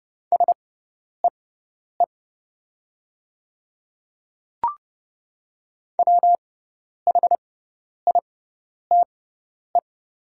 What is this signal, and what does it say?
Morse code, heard as a keyed tone, about 29 wpm, spaced as for 10 wpm, 690 Hz. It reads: SEE WHITE
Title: 6x Character Spacing - Top 300 Words - 30wpm